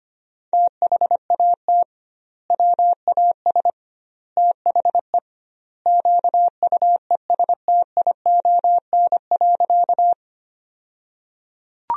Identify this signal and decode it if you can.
Morse code, heard as a keyed tone, about 25 wpm, 700 Hz. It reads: THAT WAS THE QUESTION.
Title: 2x Word-Spacing - Sentences from Top 400 Words - 25wpm